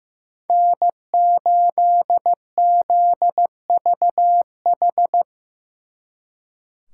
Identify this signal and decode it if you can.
Morse code, heard as a keyed tone, about 18 wpm, 700 Hz. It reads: N8ZVH
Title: Speed Racing - USA 1x3 Call Signs 15wpm